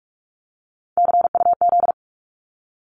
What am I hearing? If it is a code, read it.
Morse code, heard as a keyed tone, about 46 wpm, 700 Hz. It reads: CUZ
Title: Abbreviations 45wpm